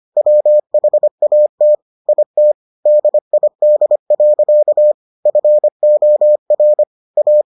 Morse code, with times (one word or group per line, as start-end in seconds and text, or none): 0.2-1.8 WHAT
2.1-2.5 IT
2.9-4.9 DID.
5.3-6.8 FOR
7.2-7.4 A